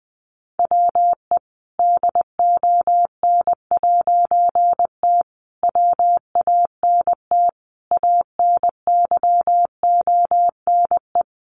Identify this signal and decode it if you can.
Morse code, heard as a keyed tone, about 20 wpm, 700 Hz. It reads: WE DON'T WANT ANYONE
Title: A Sound Of Thunder 20wpm